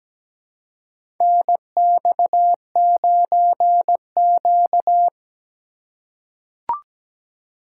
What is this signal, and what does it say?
Morse code, heard as a keyed tone, about 17 wpm, 700 Hz. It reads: NX9Q
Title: USA Call Signs - 4 Characters - 3x Character Spacing - 17wpm